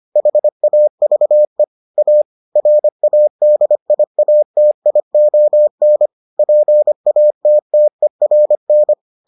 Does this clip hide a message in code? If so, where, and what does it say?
Yes: Morse code from 0.2 to 8.9 s, heard as a keyed tone, about 25 wpm, 600 Hz. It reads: HAVE A RADIATION PATTERN